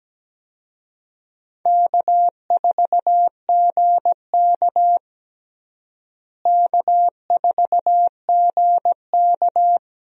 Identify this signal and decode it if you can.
Morse code, heard as a keyed tone, about 17 wpm, 700 Hz. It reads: K4GK K4GK
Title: USA Call Signs - 4 Characters - Repeated 3x - 17wpm